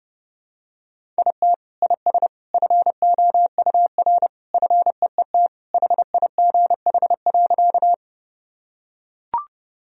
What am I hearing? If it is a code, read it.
Morse code, heard as a keyed tone, about 30 wpm, 700 Hz. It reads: IT IS FOUR FEET HIGH.
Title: Sentences from Top 1000 Words - Review Part 1 - 30wpm